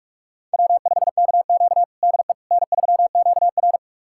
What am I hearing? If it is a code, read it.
Morse code, heard as a keyed tone, about 44 wpm, 690 Hz. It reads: W5KQ DE N3QR
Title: USA Callsign DE Callsign - 4 Characters - 45wpm